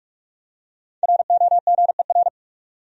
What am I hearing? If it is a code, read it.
Morse code, heard as a keyed tone, about 42 wpm, 700 Hz. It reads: ROGER
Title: USA Common Names - Max 5 Letters - 45wpm